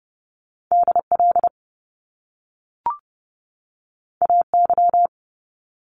Morse code, heard as a keyed tone, about 30 wpm, 700 Hz. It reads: DL AY
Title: Top Two Letter Combinations in Words 30wpm